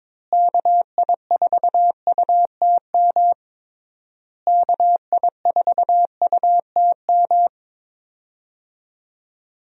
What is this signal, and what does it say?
Morse code, heard as a keyed tone, about 21 wpm, 700 Hz. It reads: KI4UTM KI4UTM
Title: USA Call Signs - 6 Characters - Repeated 3x - 22wpm